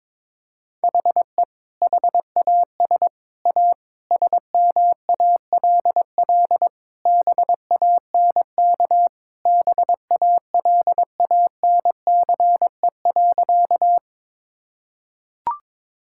Morse code, heard as a keyed tone, about 21 wpm, 700 Hz. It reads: HE HAS A SMALL BANK BALANCE.
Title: Sentences from Top 1000 Words - Review Part 6 - 22wpm